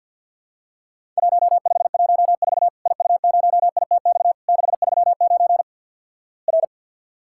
Morse code, heard as a keyed tone, about 50 wpm, 700 Hz. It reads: 1514 IU0ITX 639 R
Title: Worked All Europe DX Contest - QTC Exchange with Repeats - 50wpm